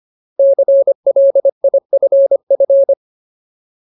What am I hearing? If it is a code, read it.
Morse code, heard as a keyed tone, about 25 wpm, 550 Hz. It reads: CLIFF
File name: Ultimate Word List / 25WPM Ultimate